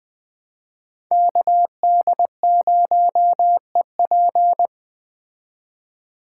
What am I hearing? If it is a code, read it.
Morse code, heard as a keyed tone, about 21 wpm, 700 Hz. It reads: KD0EP